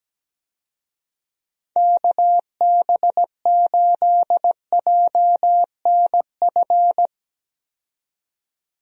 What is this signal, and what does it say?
Morse code, heard as a keyed tone, about 17 wpm, 700 Hz. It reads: KB8JNF
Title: USA Call Signs - 6 Characters - 17wpm